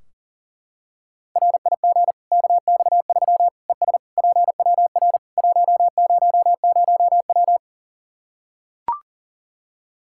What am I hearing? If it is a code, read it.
Morse code, heard as a keyed tone, about 40 wpm, 700 Hz. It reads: RIG KX3 ES PWR 100W
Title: K1USN SST QSO Practice 40wpm